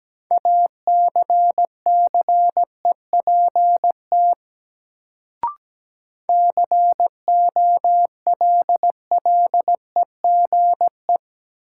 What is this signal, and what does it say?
Morse code, heard as a keyed tone, about 17 wpm, 700 Hz. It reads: ACCEPT COLLEGE
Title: Sentences from Top 1300 Words - New Word Drill - 17wpm